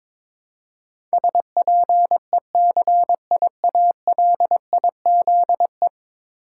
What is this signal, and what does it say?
Morse code, heard as a keyed tone, about 22 wpm, 700 Hz. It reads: SPECIALIZE
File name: practice / 22wpm / 10 Letter Words - First 3 Letters Given - 22wpm